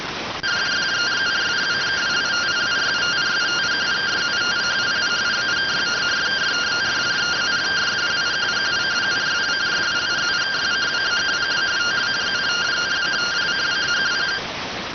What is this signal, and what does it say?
Two-tone teleprinter signal, about 170 Hz shift.